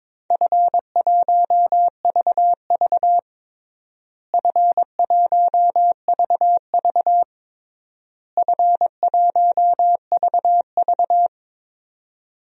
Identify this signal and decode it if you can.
Morse code, heard as a keyed tone, about 22 wpm, 700 Hz. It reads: F1VV F1VV F1VV
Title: DXCC-100 Call Signs - 4 Characters - Repeated 3x - 22wpm